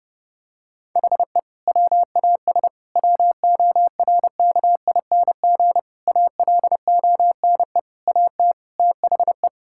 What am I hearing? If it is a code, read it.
Morse code, heard as a keyed tone, about 30 wpm, 700 Hz. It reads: HE WAS WORKING ALONE AT THE
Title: Sentences from Top 1000 Words - Review Part 5 - 30wpm